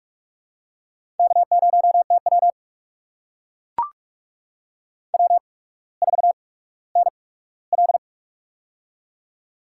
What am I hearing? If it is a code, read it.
Morse code, heard as a keyed tone, about 44 wpm, spaced as for 19 wpm, 690 Hz. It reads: K0TW W4NL
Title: USA Call Signs - 4 Characters - 8x Character Spacing - 45wpm